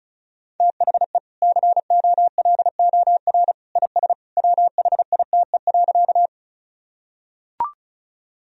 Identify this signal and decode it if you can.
Morse code, heard as a keyed tone, about 36 wpm, 700 Hz. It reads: THE COLOR IS WHITE.
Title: Sentences from Top 1000 Words - Review Part 2 - 35wpm